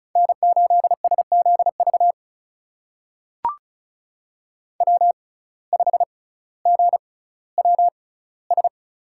Morse code, heard as a keyed tone, about 36 wpm, spaced as for 18 wpm, 700 Hz. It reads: N8SZV W5GWS
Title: USA Call Signs - 5 Characters - 6x Character Spacing - 35wpm